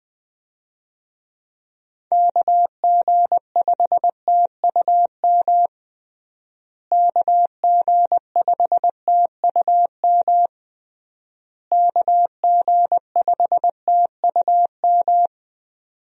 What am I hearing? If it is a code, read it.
Morse code, heard as a keyed tone, about 20 wpm, 700 Hz. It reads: KG5TUM KG5TUM KG5TUM
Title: USA Call Signs - 6 Characters - Repeated 3x - 20wpm